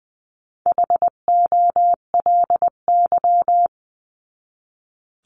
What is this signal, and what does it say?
Morse code, heard as a keyed tone, about 20 wpm, 700 Hz. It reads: HOLY
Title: 259 Lesson 42d - Words 18-20wpm